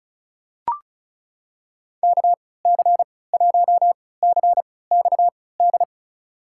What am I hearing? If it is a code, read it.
Morse code, heard as a keyed tone, about 36 wpm, spaced as for 20 wpm, 700 Hz. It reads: KC1CXD
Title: USA Call Signs - 6 Characters - 3x Character Spacing - 35wpm